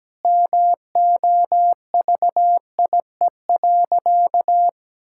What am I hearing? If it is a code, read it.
Morse code, heard as a keyed tone, about 17 wpm, 700 Hz. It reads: MOVIE.
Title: Sentences from Top 1000 Words - Review Part 5 - 17wpm